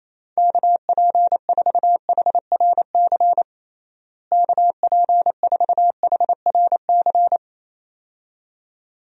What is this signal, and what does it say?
Morse code, heard as a keyed tone, about 28 wpm, 700 Hz. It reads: KP4HRC KP4HRC
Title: USA Call Signs - 6 Characters - Repeated 2x - 28wpm